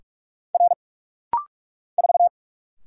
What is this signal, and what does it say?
Morse code, heard as a keyed tone, about 44 wpm, 700 Hz. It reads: R 4
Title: Single Letter-Number Alt - Rapid-Fire 45wpm